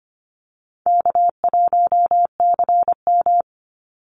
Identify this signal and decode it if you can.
Morse code, heard as a keyed tone, about 25 wpm, 700 Hz. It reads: K1CM
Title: NA SOTA Chaser Callsigns 25wpm